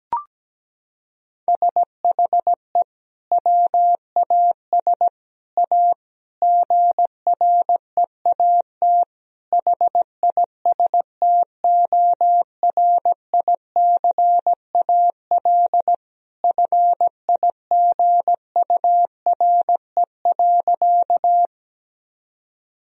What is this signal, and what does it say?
Morse code, heard as a keyed tone, about 17 wpm, 700 Hz. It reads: SHE WAS A GREAT HISTORICAL FIGURE.
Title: Sentences from Top 1000 Words - Review Part 4 - 17wpm